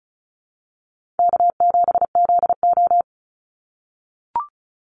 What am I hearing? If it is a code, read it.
Morse code, heard as a keyed tone, about 34 wpm, 700 Hz. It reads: K7ZO
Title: NA SOTA Chaser Callsigns 35wpm